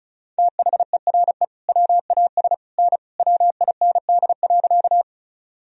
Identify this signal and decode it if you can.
Morse code, heard as a keyed tone, about 34 wpm, 700 Hz. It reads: THERE WAS N WIND.
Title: Sentences from Top 400 Words and Abbreviations 35wpm